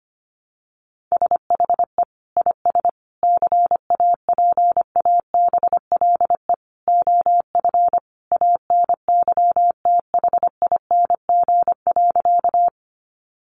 Morse code, heard as a keyed tone, about 25 wpm, 700 Hz. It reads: SHE IS CAPABLE OF ANYTHING.
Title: Sentences from Top 1200 Words 25wpm